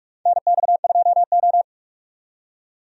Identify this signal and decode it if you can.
Morse code, heard as a keyed tone, about 45 wpm, 700 Hz. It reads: NX2O
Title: USA Call Signs - 4 Characters - 45wpm